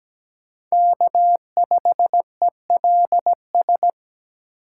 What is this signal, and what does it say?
Morse code, heard as a keyed tone, about 17 wpm, 700 Hz. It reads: K5ELS
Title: USA Call Signs - 5 Characters - 8x Character Spacing - 17wpm